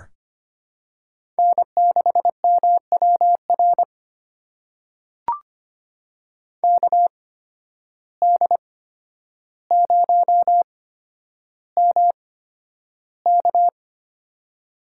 Morse code, heard as a keyed tone, about 25 wpm, spaced as for 11 wpm, 700 Hz. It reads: N6MWR KD0MK
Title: USA Call Signs - 5 Characters - 8x Character Spacing - 25wpm